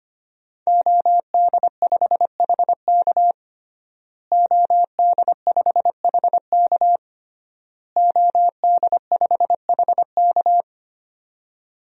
Morse code, heard as a keyed tone, about 25 wpm, 700 Hz. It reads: OD5HK OD5HK OD5HK
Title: DXCC-100 Call Signs - 5 Characters - Repeated 3x - 25wpm